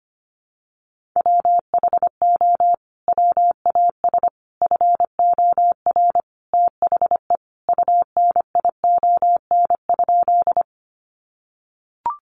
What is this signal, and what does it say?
Morse code, heard as a keyed tone, about 25 wpm, 700 Hz. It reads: WHO WAS FOR THE UNION?
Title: Sentences from Top 1400 Words 25wpm